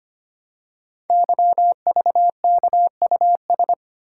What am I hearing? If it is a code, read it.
Morse code, heard as a keyed tone, about 25 wpm, 700 Hz. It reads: YVKUS
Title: Code Groups - 5 Letters - 25wpm